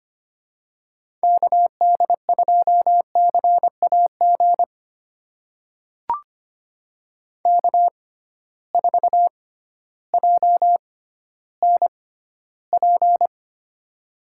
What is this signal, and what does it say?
Morse code, heard as a keyed tone, about 25 wpm, spaced as for 14 wpm, 700 Hz. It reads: KD2CAG K4JNP
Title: USA Call Signs - 6x Character Spacing - 25wpm